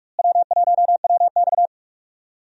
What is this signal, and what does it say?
Morse code, heard as a keyed tone, about 44 wpm, 700 Hz. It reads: W1WX
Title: USA Call Signs - 4 Characters - Repeated 3x - 45wpm